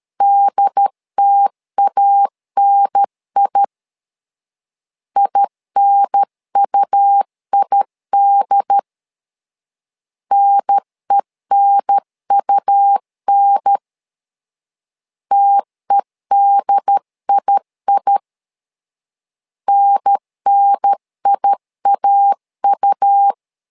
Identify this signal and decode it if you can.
Morse code, heard as a keyed tone, about 13 wpm, 800 Hz. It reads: DTANI INUID NENUN TEDII NNIAU
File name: LEZIONI / 2 UHDN / 2 UHDN 12wpm 1456